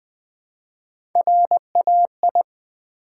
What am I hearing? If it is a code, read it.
Morse code, heard as a keyed tone, about 20 wpm, 700 Hz. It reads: RAI
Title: Build A Word - 1 Letter At A Time - 5-Letter Words - 20wpm